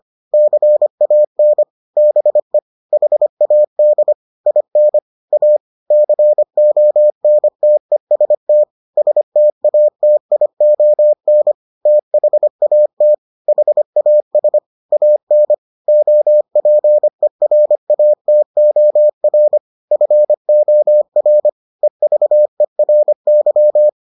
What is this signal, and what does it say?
Morse code, heard as a keyed tone, about 25 wpm, 600 Hz. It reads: CAN BE HAD IN A CONTEST STATION THAT HAS AN OPERATOR FOR EVERY